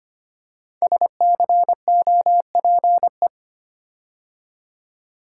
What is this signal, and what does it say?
Morse code, heard as a keyed tone, about 25 wpm, 700 Hz. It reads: SCOPE
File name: 5 Letter Words - First 3 Letters Given - 25wpm